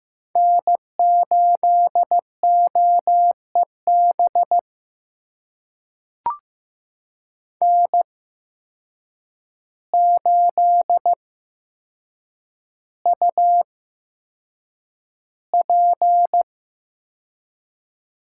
Morse code, heard as a keyed tone, about 15 wpm, spaced as for 7 wpm, 700 Hz. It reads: N8OEB N8UP
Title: USA Call Signs - 5 Characters - 8x Character Spacing - 15wpm